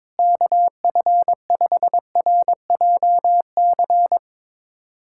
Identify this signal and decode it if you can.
Morse code, heard as a keyed tone, about 22 wpm, 700 Hz. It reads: KF5RJC
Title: USA Callsign DE Callsign 22wpm